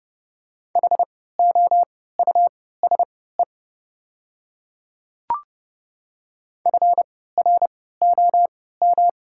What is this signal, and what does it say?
Morse code, heard as a keyed tone, about 30 wpm, spaced as for 17 wpm, 690 Hz. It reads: HOUSE FROM
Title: 3x Character Spacing - Top 200 Words - 30wpm